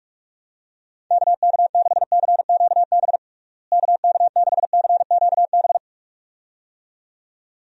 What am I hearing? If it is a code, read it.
Morse code, heard as a keyed tone, about 44 wpm, 700 Hz. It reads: KK6CQB KK6CQB
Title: USA Call Signs - 6 Characters - Repeated 2x - 45wpm